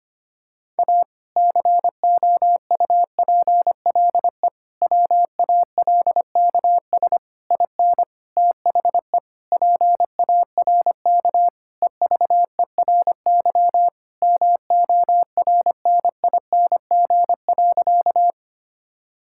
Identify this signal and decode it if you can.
Morse code, heard as a keyed tone, about 25 wpm, 700 Hz. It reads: A COUPLE WALKS IN THE PARK EVERY MORNING.
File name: practice / 25wpm / Sentences with 2-Word Phrases - 25wpm